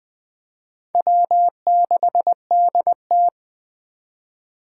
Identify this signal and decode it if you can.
Morse code, heard as a keyed tone, about 20 wpm, 700 Hz. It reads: W6DT
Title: 2023 SOTA Chaser Call Signs - 20wpm